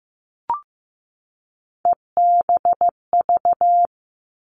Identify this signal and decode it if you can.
Morse code, heard as a keyed tone, about 15 wpm, 700 Hz. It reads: EBV
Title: Three Challenging Letter-Number Combinations 15wpm